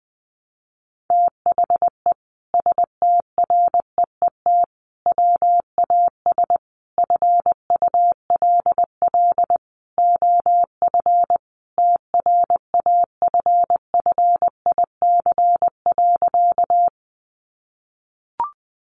Morse code, heard as a keyed tone, about 20 wpm, 700 Hz. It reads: THE STREET WAS FULL OF TRAFFIC.
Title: Sentences from Top 1300 Words 20wpm